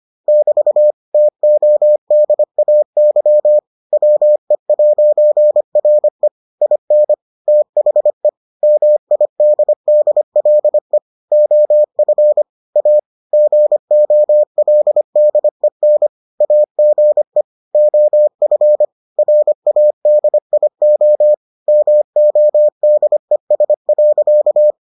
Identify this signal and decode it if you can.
Morse code, heard as a keyed tone, about 25 wpm, 600 Hz. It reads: <BT> TODAY WE'RE IN THE MIDDLE OF A GOLDEN AGE OF RADIO MODES.